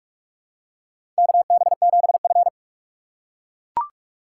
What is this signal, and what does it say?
Morse code, heard as a keyed tone, about 44 wpm, 690 Hz. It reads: KB7F
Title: USA Call Signs - 4 Characters - 2x Character Spacing - 45wpm